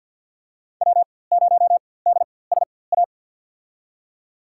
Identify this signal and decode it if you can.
Morse code, heard as a keyed tone, about 52 wpm, spaced as for 23 wpm, 700 Hz. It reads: W0DSA
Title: USA Call Signs - 4x Character Spacing - 50wpm